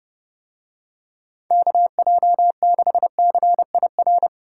Morse code, heard as a keyed tone, about 28 wpm, 700 Hz. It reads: KJ6CIR